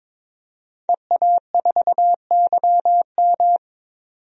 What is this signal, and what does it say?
Morse code, heard as a keyed tone, about 22 wpm, 700 Hz. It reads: EA4YM
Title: DXCC-100 Call Signs - 5 Characters - 22wpm